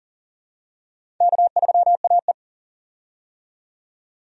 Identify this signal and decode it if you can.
Morse code, heard as a keyed tone, about 40 wpm, 700 Hz. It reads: K3AE
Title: USA Call Signs - 4 Characters - 40wpm